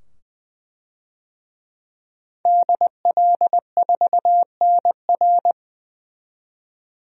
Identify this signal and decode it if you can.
Morse code, heard as a keyed tone, about 20 wpm, 700 Hz. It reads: DL4NR
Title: DXCC-100 Call Signs - 5 Characters - 20wpm